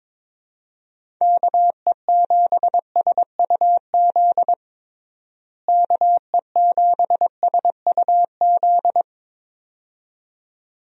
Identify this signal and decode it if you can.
Morse code, heard as a keyed tone, about 22 wpm, 700 Hz. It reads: KE7SUZ KE7SUZ
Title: USA Call Signs - 6 Characters - Repeated 2x - 22wpm